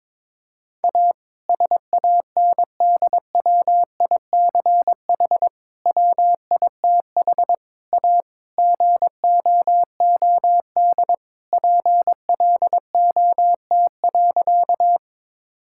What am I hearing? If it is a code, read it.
Morse code, heard as a keyed tone, about 22 wpm, 700 Hz. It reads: A SANDWICH WITH A GOOD PLOT.
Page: morse-code-ninja-practice-22wpm